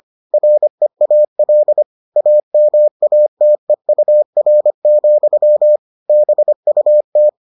Morse code, heard as a keyed tone, about 25 wpm, 600 Hz. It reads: REAL AMATEUR, BUT